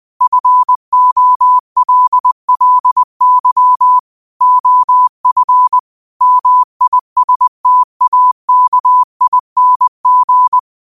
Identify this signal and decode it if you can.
Morse code, heard as a keyed tone, about 20 wpm, 1000 Hz. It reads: FOLLY OF MISTAKING
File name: qotdfiles / 20wpm